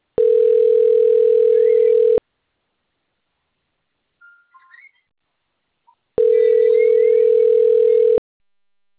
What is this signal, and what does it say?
Ringback tone (North America), 440 + 480 Hz.